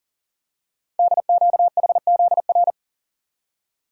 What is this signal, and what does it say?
Morse code, heard as a keyed tone, about 42 wpm, 700 Hz. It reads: DQHZR
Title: Code Groups - 5 Letter:Numbers - 40wpm